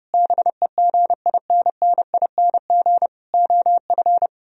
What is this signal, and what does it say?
Morse code, heard as a keyed tone, about 38 wpm, 700 Hz. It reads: BEGINNING OF